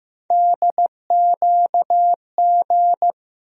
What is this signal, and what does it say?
Morse code, heard as a keyed tone, about 15 wpm, 700 Hz. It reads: DQG
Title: USA Call Sign Postfixes - 3 Letters 15wpm